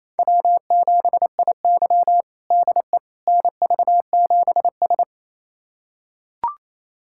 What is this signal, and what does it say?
Morse code, heard as a keyed tone, about 28 wpm, 700 Hz. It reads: W7IY DE NV7S
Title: USA Callsign DE Callsign - 4 Characters - 28wpm